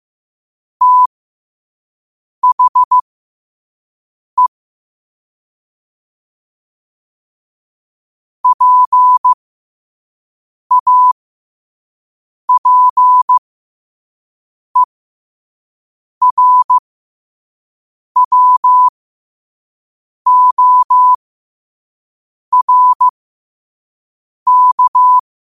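Morse code, heard as a keyed tone, about 15 wpm, spaced as for 5 wpm, 1000 Hz. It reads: THE PAPERWORK
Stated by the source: Morse code sent at 5 wpm.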